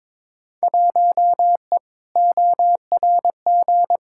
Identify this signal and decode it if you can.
Morse code, heard as a keyed tone, about 21 wpm, 700 Hz. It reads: 1E ORG
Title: Field Day Exchanges - 22wpm